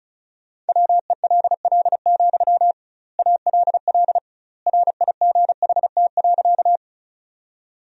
Morse code, heard as a keyed tone, about 34 wpm, 700 Hz. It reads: WELL, ALL RIGHT.
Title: 2x Word-Spacing - Sentences from Top 200 Words - 35wpm